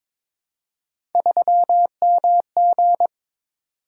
Morse code, heard as a keyed tone, about 22 wpm, 700 Hz. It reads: 3MG
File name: practice / 22wpm / Three Letter-Number Combinations - 22wpm